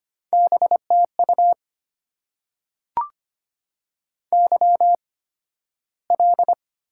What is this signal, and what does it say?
Morse code, heard as a keyed tone, about 25 wpm, spaced as for 13 wpm, 700 Hz. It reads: BTU YL
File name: Top 90 QSO Elements - 8x Character Spacing - 25wpm